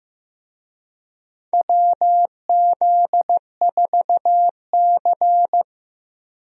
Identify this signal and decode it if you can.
Morse code, heard as a keyed tone, about 15 wpm, 700 Hz. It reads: WZ4C
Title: USA Call Signs - 4 Characters - Repeated 2x - 15wpm